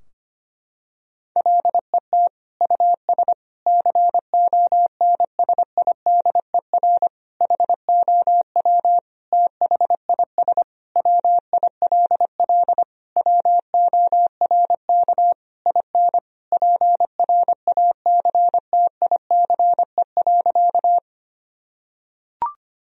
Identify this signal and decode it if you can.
Morse code, heard as a keyed tone, about 25 wpm, 700 Hz. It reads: LET US CONSIDER HOW THIS WILL WORK IN PRACTICE.
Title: Sentences from Top 1000 Words - Review Part 4 - 25wpm